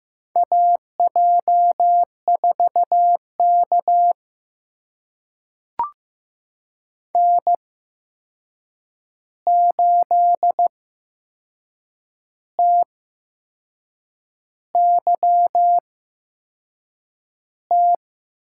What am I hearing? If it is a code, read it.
Morse code, heard as a keyed tone, about 15 wpm, spaced as for 6 wpm, 700 Hz. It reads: AJ4K N8TYT
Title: USA Call Signs - 8x Character Spacing - 15wpm